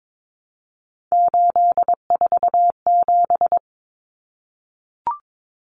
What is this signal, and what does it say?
Morse code, heard as a keyed tone, about 22 wpm, 700 Hz. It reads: 847
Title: Numbers - 3-Digit 22wpm